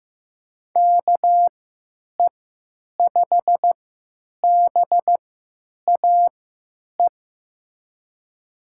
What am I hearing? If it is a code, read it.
Morse code, heard as a keyed tone, about 15 wpm, spaced as for 9 wpm, 700 Hz. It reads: KE5BAE
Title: USA Call Signs - 3x Character Spacing - 15wpm